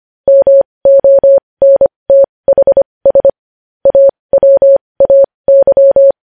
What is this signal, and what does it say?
Morse code, heard as a keyed tone, about 25 wpm, spaced as for 20 wpm, 550 Hz. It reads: MONTHS AWAY
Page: News headlines in Morse code at 20 WPM